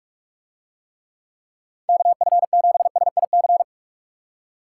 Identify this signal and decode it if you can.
Morse code, heard as a keyed tone, about 46 wpm, 700 Hz. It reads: KF7SIC